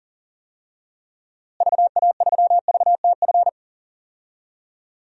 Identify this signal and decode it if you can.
Morse code, heard as a keyed tone, about 40 wpm, 700 Hz. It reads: VA3VTF
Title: DXCC-100 Call Signs - 6 Characters - Repeated 2x - 40wpm